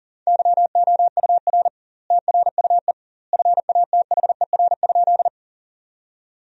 Morse code, heard as a keyed tone, about 40 wpm, 700 Hz. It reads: YOUR TRUE FATHER?
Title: 2x Word-Spacing - Sentences from Top 400 Words - 40wpm